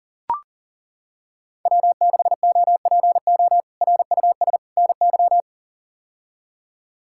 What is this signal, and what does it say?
Morse code, heard as a keyed tone, about 40 wpm, 700 Hz. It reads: W6OPO RUS NY
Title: K1USN SST QSO Practice - 40wpm